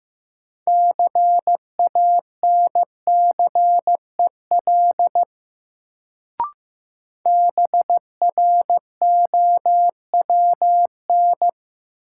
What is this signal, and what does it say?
Morse code, heard as a keyed tone, about 15 wpm, 700 Hz. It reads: CANCEL BROWN
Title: Sentences from Top 1200 Words - New Word Drill - 15wpm